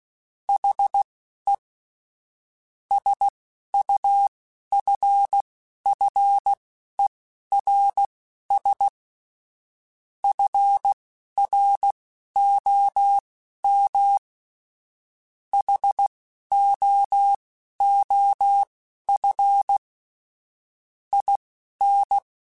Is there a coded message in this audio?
Yes: Morse code from 0.5 to 22.2 s, heard as a keyed tone, about 16 wpm, spaced as for 11 wpm, 800 Hz. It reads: HE SUFFERS FROM HOOF IN